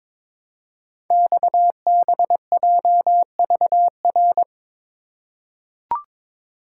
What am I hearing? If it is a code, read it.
Morse code, heard as a keyed tone, about 23 wpm, 690 Hz. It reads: XBJVR
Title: Code Groups - 5 Letters - 22wpm